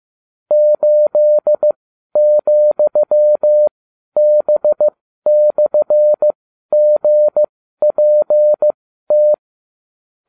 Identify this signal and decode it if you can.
Morse code, heard as a keyed tone, about 15 wpm, spaced as for 12 wpm, 600 Hz. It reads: 8,B/GPT